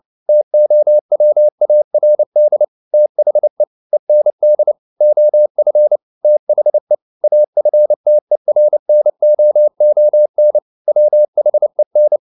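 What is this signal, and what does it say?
Morse code, heard as a keyed tone, about 29 wpm, 600 Hz. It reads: TOWARD THE END OF THE AFTERNOON WHEN
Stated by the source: Morse code at 29 wpm, sending Ken Follett's novel Never in English, from CW med Gnister.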